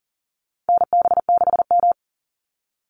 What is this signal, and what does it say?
Morse code, heard as a keyed tone, about 40 wpm, 700 Hz. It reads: NB6M
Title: Call Signs Repeated 2x - 40wpm